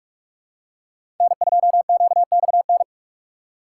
Morse code, heard as a keyed tone, about 46 wpm, 700 Hz. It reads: N2QXN
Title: USA Call Signs - 5 Characters - 3x Character Spacing - 45wpm